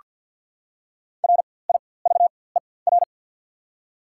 Morse code, heard as a keyed tone, about 48 wpm, spaced as for 23 wpm, 700 Hz. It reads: RIVER
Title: 4x Character Spacing - Top 500 Words - 50wpm